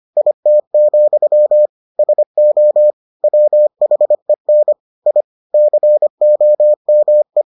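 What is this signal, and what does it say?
Morse code, heard as a keyed tone, about 25 wpm, 600 Hz. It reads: IT, SO WHEN I COME
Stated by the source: Morse edition of the Foundations of Amateur Radio episodes.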